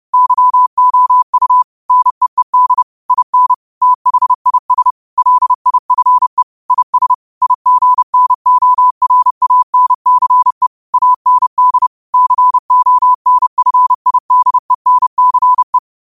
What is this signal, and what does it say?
Morse code, heard as a keyed tone, about 30 wpm, 1000 Hz. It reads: YOU NEED IN THIS LIFE IS IGNORANCE AND CONFIDENCE
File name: qotdfiles / 30wpm